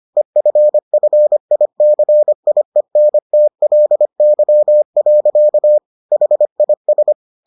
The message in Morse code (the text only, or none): EFFICIENTLY. HIS